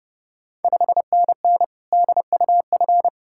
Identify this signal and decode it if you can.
Morse code, heard as a keyed tone, about 30 wpm, 690 Hz. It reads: 5NN DUF
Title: Ontario QSO Party Exchange 30wpm